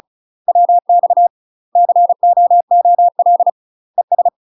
Morse code, heard as a keyed tone, about 34 wpm, 700 Hz. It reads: WX COOL ES
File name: UK MASTER QSOs TRAINING / 35WPM / UK MASTER QSOs 0030